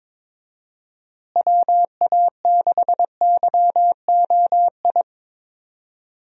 Morse code, heard as a keyed tone, about 22 wpm, 700 Hz. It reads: WA6YOI
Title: USA Call Signs - 6 Characters - Repeated 3x - 22wpm